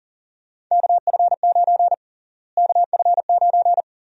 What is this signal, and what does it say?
Morse code, heard as a keyed tone, about 40 wpm, 700 Hz. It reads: KF9 KF9
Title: USA Call Sign Prefixes - Repeated 2x - 40wpm